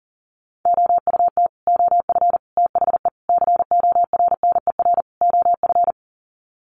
Morse code, heard as a keyed tone, about 40 wpm, 700 Hz. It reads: OUT OF THE CORNER OF
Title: Sets of 5 Words 40wpm